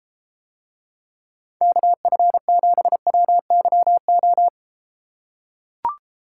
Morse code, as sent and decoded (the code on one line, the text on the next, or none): -.- ..-. --... .-- -.-- ---
KF7WYO